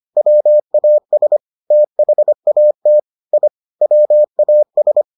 Morse code, heard as a keyed tone, about 25 wpm, 600 Hz. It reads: WAS THAT I WAS